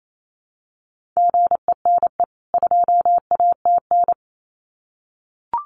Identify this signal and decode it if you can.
Morse code, heard as a keyed tone, about 28 wpm, 700 Hz. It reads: GENE 2ATN
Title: CWT Exchange 28wpm